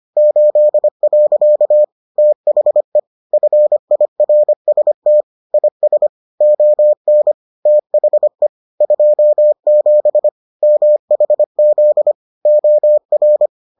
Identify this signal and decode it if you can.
Morse code, heard as a keyed tone, about 25 wpm, 600 Hz. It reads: 8. THE FIRST IS ON THE 27 MHZ OR